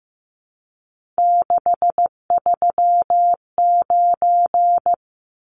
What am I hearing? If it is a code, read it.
Morse code, heard as a keyed tone, about 15 wpm, 700 Hz. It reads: 639
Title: Numbers - 3-Digit 15wpm